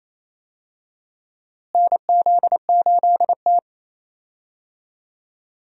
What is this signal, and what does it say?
Morse code, heard as a keyed tone, about 28 wpm, 700 Hz. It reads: NZ8T
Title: USA Call Signs - 4 Characters - Repeated 3x - 28wpm